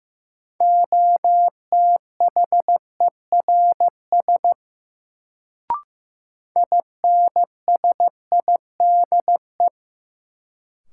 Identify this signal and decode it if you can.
Morse code, heard as a keyed tone, about 15 wpm, 700 Hz. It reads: OTHERS INSIDE
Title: Sentences from Top 1000 Words - Review Part 2 - New Word Drill - 15wpm